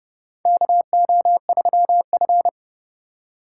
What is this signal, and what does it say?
Morse code, heard as a keyed tone, about 32 wpm, 700 Hz. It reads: KO3F
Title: USA Call Signs - 4 Characters - 2x Character Spacing - 30wpm